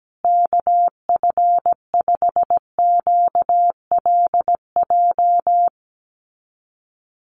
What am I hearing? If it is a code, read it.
Morse code, heard as a keyed tone, about 17 wpm, 700 Hz. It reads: KF5QLJ
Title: USA Call Signs Repeated 3x 17wpm